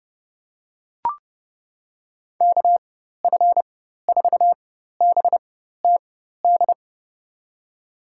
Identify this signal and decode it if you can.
Morse code, heard as a keyed tone, about 32 wpm, spaced as for 14 wpm, 700 Hz. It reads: KF4BTD
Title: USA Call Signs - 6 Characters - 4x Character Spacing - 30wpm